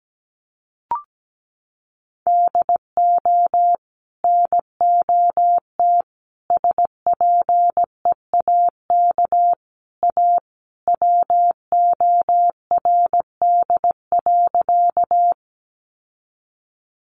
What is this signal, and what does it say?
Morse code, heard as a keyed tone, about 17 wpm, 700 Hz. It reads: DO NOT SPEAK A WORD.
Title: Sentences from Top 1100 Words 17wpm